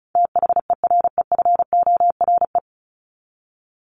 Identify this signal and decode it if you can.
Morse code, heard as a keyed tone, about 34 wpm, 700 Hz. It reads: THEREFORE
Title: Top 501-750 Words 35wpm